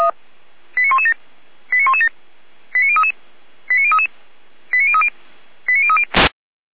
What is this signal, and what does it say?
Five-tone selective call (ZVEI), heard as quick runs of five tones, tones of about 80 ms. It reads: 89117 89117 89033 89033 89030 89030
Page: /455/SDR/ZVEI/